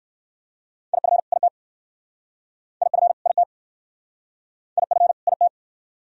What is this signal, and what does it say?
Morse code, heard as a keyed tone, about 67 wpm, 700 Hz. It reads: IF IT IF IT IF IT